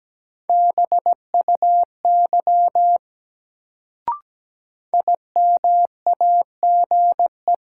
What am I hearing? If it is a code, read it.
Morse code, heard as a keyed tone, about 17 wpm, 700 Hz. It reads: BUY IMAGE
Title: Sentences from Top 700 Words - New Word Drill - 17wpm